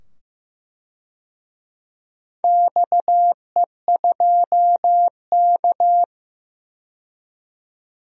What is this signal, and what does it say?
Morse code, heard as a keyed tone, about 15 wpm, 700 Hz. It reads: XE2K
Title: DXCC-100 Call Signs - 4 Characters - 15wpm